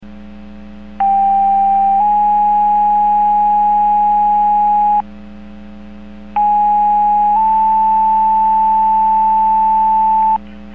Two-tone page, heard as a one-second tone, then a three-second tone, about 790 then 830 Hz, about 830 then 880 Hz. It is heard from 1.0 to 10.4 s.